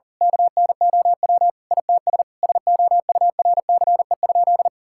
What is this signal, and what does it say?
Morse code, heard as a keyed tone, about 40 wpm, 700 Hz. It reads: KNOW ITS SOURCE?
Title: Sentences from Top 1000 Words - Review Part 5 - 40wpm